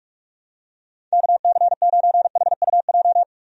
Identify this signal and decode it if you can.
Morse code, heard as a keyed tone, about 46 wpm, 700 Hz. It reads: KC9HUJ
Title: USA Call Signs - 6 Characters - 45wpm